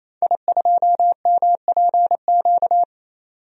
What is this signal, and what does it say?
Morse code, heard as a keyed tone, about 27 wpm, 700 Hz. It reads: I2MPQ
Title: DXCC-100 Call Signs - 5 Characters - 28wpm